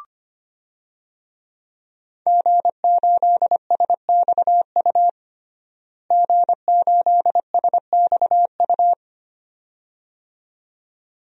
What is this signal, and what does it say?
Morse code, heard as a keyed tone, about 25 wpm, 700 Hz. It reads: G8SXU G8SXU
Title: DXCC-100 Call Signs - 5 Characters - Repeated 2x - 25wpm